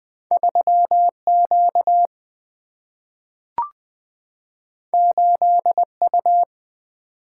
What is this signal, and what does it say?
Morse code, heard as a keyed tone, about 20 wpm, 700 Hz. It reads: 3Q 8U
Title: Two Letter-Number Combinations - 20wpm